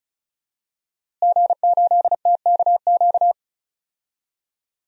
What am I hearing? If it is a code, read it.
Morse code, heard as a keyed tone, about 34 wpm, 700 Hz. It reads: G8TKQ
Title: DXCC-100 Call Signs - 5 Characters - Repeated 2x - 35wpm